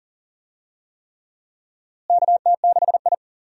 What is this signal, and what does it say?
Morse code, heard as a keyed tone, about 40 wpm, 700 Hz. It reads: KT6I
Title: USA Call Signs - 4 Characters - Repeated 3x - 40wpm